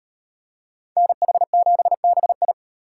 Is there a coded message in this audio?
Yes: Morse code from 1.0 to 2.5 s, heard as a keyed tone, about 38 wpm, 700 Hz. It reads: NH7BI